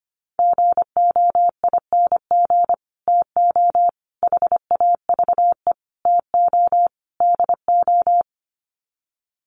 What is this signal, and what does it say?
Morse code, heard as a keyed tone, about 25 wpm, 700 Hz. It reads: GOING TO HAVE TO DO
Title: Sets of 5 Words Encore 25wpm